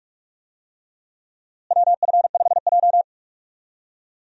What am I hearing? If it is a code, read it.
Morse code, heard as a keyed tone, about 46 wpm, 700 Hz. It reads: WF5J
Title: USA Call Signs - 4 Characters - Repeated 2x - 45wpm